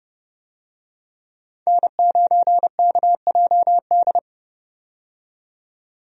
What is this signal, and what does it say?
Morse code, heard as a keyed tone, about 30 wpm, 700 Hz. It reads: N9KJD
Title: USA Call Signs - 5 Characters - 30wpm